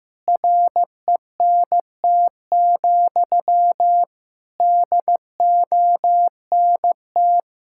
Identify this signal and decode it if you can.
Morse code, heard as a keyed tone, about 16 wpm, 700 Hz. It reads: RENT, DONT
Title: Sentences from Top 1000 Words - Review Part 5 - 15wpm